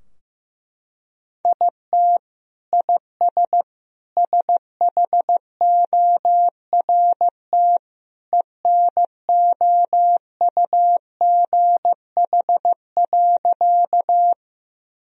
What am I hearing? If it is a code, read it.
Morse code, heard as a keyed tone, about 15 wpm, 700 Hz. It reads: IT IS SHORT ENOUGH.
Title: Sentences from Top 1000 Words - Review Part 2 - 15wpm